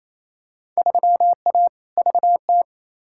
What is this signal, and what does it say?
Morse code, heard as a keyed tone, about 28 wpm, 700 Hz. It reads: 3A VT
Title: Field Day Exchanges Encore - 28wpm